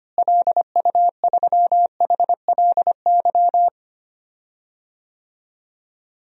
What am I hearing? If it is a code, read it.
Morse code, heard as a keyed tone, about 25 wpm, 700 Hz. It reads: LU3HLY
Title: DXCC-100 Call Signs - 6 Characters - Repeated 3x - 25wpm